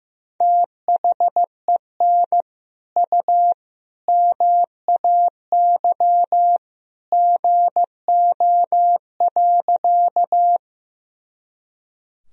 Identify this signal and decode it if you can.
Morse code, heard as a keyed tone, about 15 wpm, 700 Hz. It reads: THEN U MAY GO.
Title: Sentences from Top 100 Words and Abbreviations 15wpm